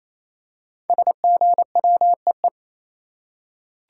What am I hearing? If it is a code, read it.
Morse code, heard as a keyed tone, about 27 wpm, 700 Hz. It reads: SGWEE
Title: Code Groups - 5 Letters - 28wpm